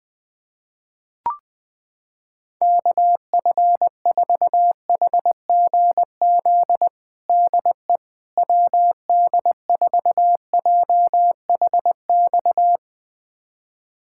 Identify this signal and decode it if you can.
Morse code, heard as a keyed tone, about 20 wpm, 700 Hz. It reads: KF4HGZ DE WD4JHX
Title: USA Callsign DE Callsign - 6 Characters - 20wpm